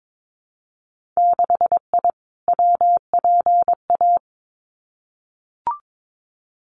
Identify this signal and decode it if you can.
Morse code, heard as a keyed tone, about 22 wpm, 700 Hz. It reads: 6I WPA
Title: Winter Field Day Exchanges Encore 22wpm